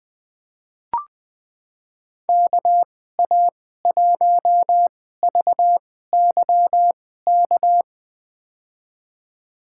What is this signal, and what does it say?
Morse code, heard as a keyed tone, about 20 wpm, spaced as for 14 wpm, 700 Hz. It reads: KA1VYK
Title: USA Call Signs - 2x Character Spacing - 20wpm